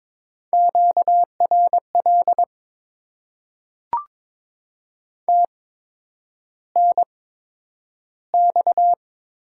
Morse code, heard as a keyed tone, about 22 wpm, spaced as for 9 wpm, 700 Hz. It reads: QRL TNX
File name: Top 20 QSO Elements - 8x Character Spacing - 22wpm